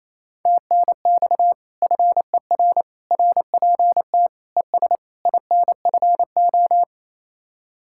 Morse code, heard as a keyed tone, about 35 wpm, 700 Hz. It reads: TNX FER RPT ES INFO
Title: Speed-Racing Common QSO Phrases - 28wpm